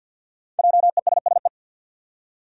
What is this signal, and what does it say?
Morse code, heard as a keyed tone, about 51 wpm, 700 Hz. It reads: JESSE
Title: USA Common Names - Max 5 Letters - 50wpm